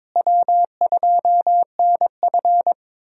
Morse code, heard as a keyed tone, about 22 wpm, 700 Hz. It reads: W2NF
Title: USA Call Signs - 4 Characters - Repeated 3x - 22wpm